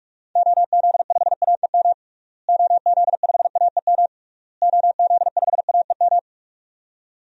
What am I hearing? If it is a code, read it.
Morse code, heard as a keyed tone, about 41 wpm, 700 Hz. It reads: OZ5AEM OZ5AEM OZ5AEM
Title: DXCC-100 Call Signs - 6 Characters - Repeated 3x - 45wpm